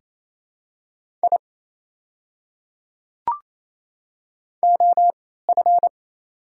Morse code, heard as a keyed tone, about 28 wpm, spaced as for 16 wpm, 690 Hz. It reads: I OF
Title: 3x Character Spacing - Top 200 Words - 28wpm